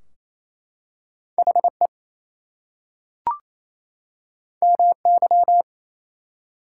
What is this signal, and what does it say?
Morse code, heard as a keyed tone, about 27 wpm, 700 Hz. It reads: HE MY
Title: Two Letter Words - 28wpm